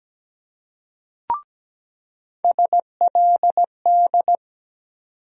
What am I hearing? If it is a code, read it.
Morse code, heard as a keyed tone, about 17 wpm, 700 Hz. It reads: SLD
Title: USA Call Sign Postfixes - 3 Letters 17wpm